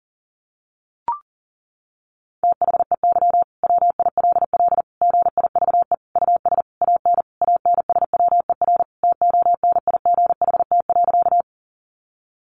Morse code, heard as a keyed tone, about 40 wpm, 700 Hz. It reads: THEY WILL GIVE US AN ANSWER TONIGHT.
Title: Sentences from Top 600 Words 40wpm